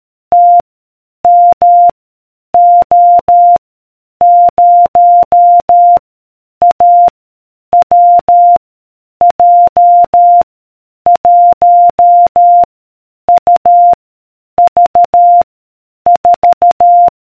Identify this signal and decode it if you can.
Morse code, heard as a keyed tone, about 13 wpm, 700 Hz. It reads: T M O 0 A W J 1 U V 4